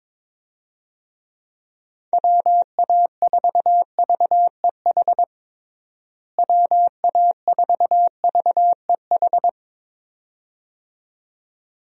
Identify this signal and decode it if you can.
Morse code, heard as a keyed tone, about 22 wpm, 700 Hz. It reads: WA4VEH WA4VEH
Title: USA Call Signs - 6 Characters - Repeated 2x - 22wpm